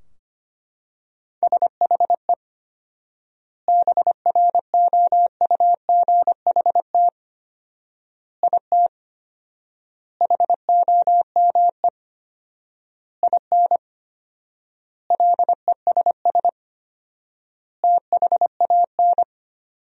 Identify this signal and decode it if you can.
Morse code, heard as a keyed tone, about 25 wpm, 700 Hz. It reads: SHE BROUGHT IT HOME IN LESS THAN
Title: 4x Word-Spacing - Sentences from Top 500 Words - 25wpm